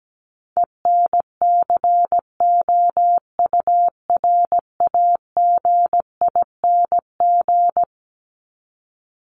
Morse code, heard as a keyed tone, about 17 wpm, 700 Hz. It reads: ENCOURAGING
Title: Top 1751-2000 Words 17wpm